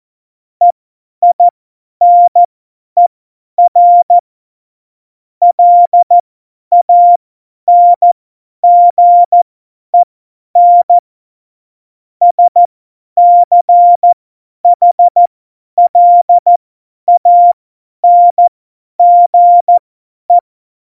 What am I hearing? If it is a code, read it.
Morse code, heard as a keyed tone, about 14 wpm, spaced as for 10 wpm, 700 Hz. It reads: EINER LANGEN SCHLANGE